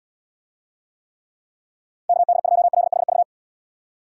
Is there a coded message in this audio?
Yes: Morse code from 2.1 to 3.2 s, heard as a keyed tone, about 76 wpm, 700 Hz. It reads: KD2LHV